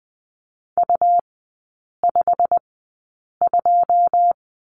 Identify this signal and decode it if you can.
Morse code, heard as a keyed tone, about 20 wpm, 700 Hz. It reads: U 5 2
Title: 107 Lesson 18e - Letter m - Three Characters 20wpm